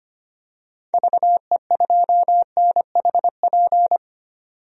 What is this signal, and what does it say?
Morse code, heard as a keyed tone, about 25 wpm, 700 Hz. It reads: VE2NHP
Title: DXCC-100 Call Signs - 6 Characters - 25wpm